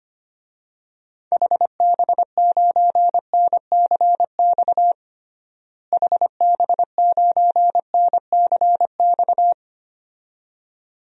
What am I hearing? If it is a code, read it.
Morse code, heard as a keyed tone, about 25 wpm, 700 Hz. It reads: HB9NCX HB9NCX
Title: DXCC-100 Call Signs - 6 Characters - Repeated 2x - 25wpm